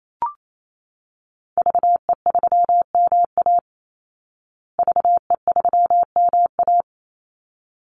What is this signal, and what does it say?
Morse code, heard as a keyed tone, about 28 wpm, 700 Hz. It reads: VE3MA VE3MA
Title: Call Signs Repeated 3x - 28wpm